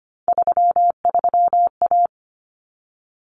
Morse code, heard as a keyed tone, about 25 wpm, 700 Hz. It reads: 33A
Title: Common Cut Numbers - 3-Digit 25wpm